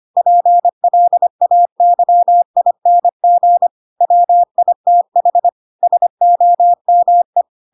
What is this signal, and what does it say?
Morse code, heard as a keyed tone, about 25 wpm, 700 Hz. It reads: PLAYING WITH SOME
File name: Pinocchio / Pinocchio ch7 25 25 ik1hge